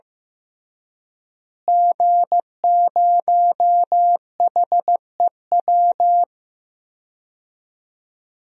Morse code, heard as a keyed tone, about 15 wpm, 700 Hz. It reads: G0HEW
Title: DXCC-100 Call Signs - 5 Characters - Repeated 3x - 15wpm